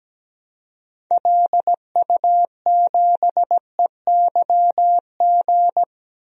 Morse code, heard as a keyed tone, about 17 wpm, 700 Hz. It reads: LU7EYG
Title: DXCC-100 Call Signs - 6 Characters - 17wpm